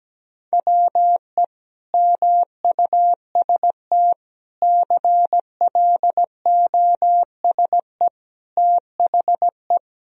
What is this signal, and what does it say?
Morse code, heard as a keyed tone, about 17 wpm, 700 Hz. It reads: WE MUST CLOSE THE
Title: Sentences from Top 1000 Words - Review Part 1 - 17wpm